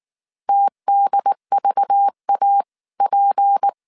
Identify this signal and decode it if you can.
Morse code, heard as a keyed tone, about 19 wpm, 800 Hz. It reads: TDVAP